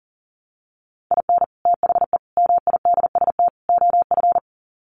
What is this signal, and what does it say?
Morse code, heard as a keyed tone, about 40 wpm, 700 Hz. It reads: IN THE MIDST OF
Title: Sets of 4 Words 40wpm